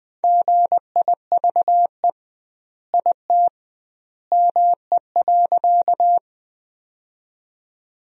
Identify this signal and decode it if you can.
Morse code, heard as a keyed tone, about 20 wpm, 700 Hz. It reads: GIVE IT ME.
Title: 2x Word-Spacing - Sentences from Top 200 Words - 20wpm